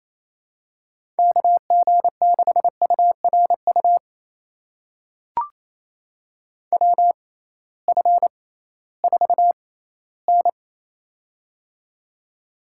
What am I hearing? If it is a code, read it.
Morse code, heard as a keyed tone, about 28 wpm, spaced as for 16 wpm, 700 Hz. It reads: KG6URU WF4N
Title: USA Call Signs - 6x Character Spacing - 28wpm